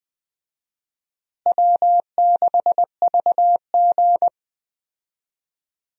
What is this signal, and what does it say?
Morse code, heard as a keyed tone, about 20 wpm, 700 Hz. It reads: W6VG
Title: USA Call Signs - 4 Characters - Repeated 2x - 20wpm